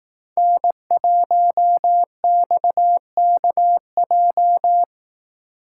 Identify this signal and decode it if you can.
Morse code, heard as a keyed tone, about 18 wpm, 700 Hz. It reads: N1XKJ
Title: Speed Racing - USA 1x3 Call Signs 15wpm